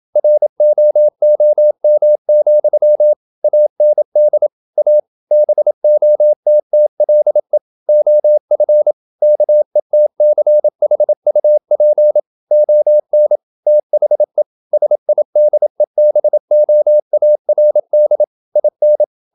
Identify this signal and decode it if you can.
Morse code, heard as a keyed tone, about 27 wpm, 600 Hz. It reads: ROOM, AND A BOTTLE OF KETCHUP ON THE SIDEBOARD IN
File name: Never 0027wpm